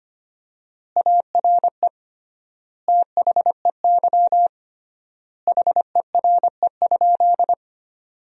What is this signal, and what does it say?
Morse code, heard as a keyed tone, about 25 wpm, 700 Hz. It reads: ARE THEY HERE?
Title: 3x Word-Spacing - Sentences from Top 200 Words - 25wpm